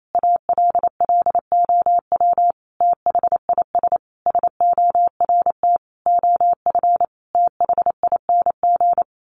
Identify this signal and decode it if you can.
Morse code, heard as a keyed tone, about 28 wpm, 700 Hz. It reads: ALLOW THIS SORT OF THING
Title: Sentences from Top 1300 Words 28wpm